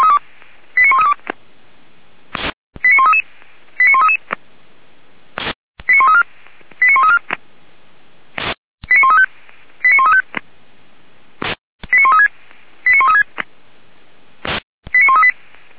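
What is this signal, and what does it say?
Five-tone selective call (ZVEI), heard as quick runs of five tones, tones of about 70 ms. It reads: 89132 89133 89133 89134 89134 89135 89135 89136 89136 89138
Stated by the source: Five-tone ZVEI selective call 89131 received on an SDR.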